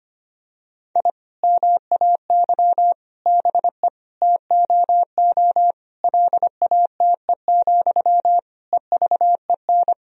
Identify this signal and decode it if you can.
Morse code, heard as a keyed tone, about 25 wpm, 700 Hz. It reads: I MAY BE TOO LATE, EVEN
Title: Sentences from Top 1000 Words - Review Part 1 - 25wpm